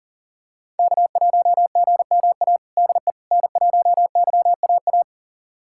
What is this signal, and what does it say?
Morse code, heard as a keyed tone, about 40 wpm, 700 Hz. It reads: K1GMA DE N1YAA
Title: USA Callsign DE Callsign - 5 Characters - 40wpm